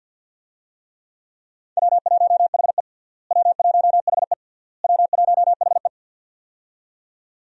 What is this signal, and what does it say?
Morse code, heard as a keyed tone, about 51 wpm, 700 Hz. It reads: W1HE W1HE W1HE
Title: USA Call Signs - 4 Characters - Repeated 3x - 50wpm